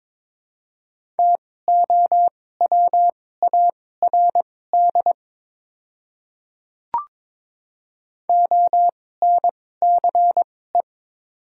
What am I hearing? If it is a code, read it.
Morse code, heard as a keyed tone, about 22 wpm, spaced as for 16 wpm, 700 Hz. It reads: TOWARD ONCE
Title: 2x Character Spacing - Top 500 Words - 22wpm